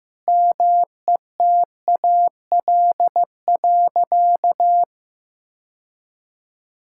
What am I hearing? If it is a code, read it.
Morse code, heard as a keyed tone, about 15 wpm, 700 Hz. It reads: METAL.
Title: Sentences from Top 1000 Words - Review Part 6 - 15wpm